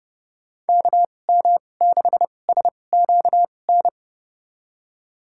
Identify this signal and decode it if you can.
Morse code, heard as a keyed tone, about 31 wpm, spaced as for 22 wpm, 700 Hz. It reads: KM6SQN
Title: USA Call Signs - 6 Characters - 2x Character Spacing - 30wpm